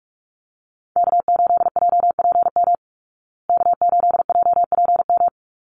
Morse code, heard as a keyed tone, about 44 wpm, 700 Hz. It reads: K8JPM K8JPM
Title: Call Signs Repeated 2x - 45wpm